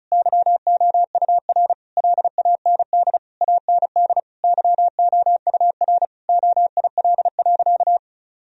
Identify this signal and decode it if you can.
Morse code, heard as a keyed tone, about 36 wpm, 700 Hz. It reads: YOUR LAND AND YOUR OIL.
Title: Sentences from Top 1000 Words - Review Part 4 - 35wpm